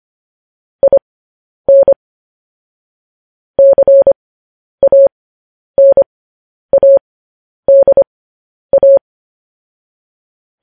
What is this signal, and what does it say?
Morse code, heard as a keyed tone, about 25 wpm, spaced as for 10 wpm, 550 Hz. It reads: IN CANADA
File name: podcast / News headlines-1x-25x10